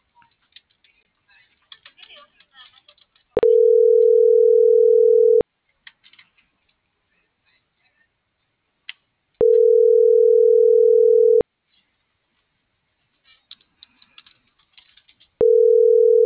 Ringback tone (North America), 440 + 480 Hz.